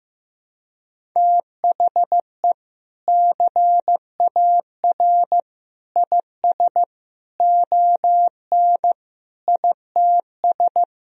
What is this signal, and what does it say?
Morse code, heard as a keyed tone, about 15 wpm, 700 Hz. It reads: THE CAR IS ON ITS
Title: Sentences from Top 1000 Words - Review Part 1 - 15wpm